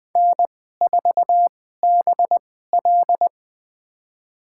Morse code, heard as a keyed tone, about 21 wpm, spaced as for 15 wpm, 700 Hz. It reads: N4BL
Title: USA Call Signs - 4 Characters - 2x Character Spacing - 20wpm